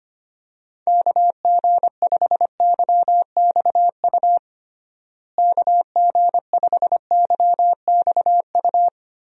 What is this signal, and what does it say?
Morse code, heard as a keyed tone, about 25 wpm, 700 Hz. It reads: KG5YXU KG5YXU
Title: USA Call Signs - 6 Characters - Repeated 3x - 25wpm